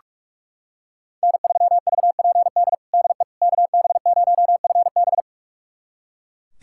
Morse code, heard as a keyed tone, about 41 wpm, 700 Hz. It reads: N3VPD DE KB0FB
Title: USA Callsign DE Callsign - 5 Characters - 45wpm